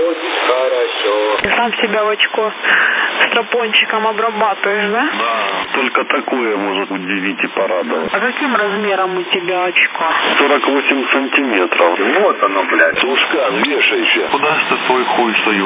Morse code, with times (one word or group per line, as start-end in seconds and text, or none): none